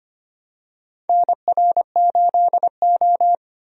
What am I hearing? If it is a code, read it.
Morse code, heard as a keyed tone, about 25 wpm, 700 Hz. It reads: NR8O
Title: USA Call Signs - 4 Characters - 25wpm